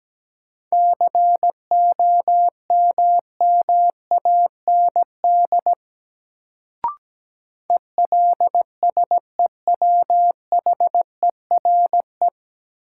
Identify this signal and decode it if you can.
Morse code, heard as a keyed tone, about 17 wpm, 700 Hz. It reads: COMMAND ELSEWHERE
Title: Sentences from Top 1600 Words - New Word Drill - 17wpm